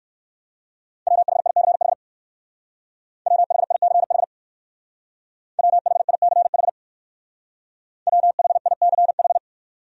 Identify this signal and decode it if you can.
Morse code, heard as a keyed tone, about 58 wpm, 700 Hz. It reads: WHICH WHICH WHICH WHICH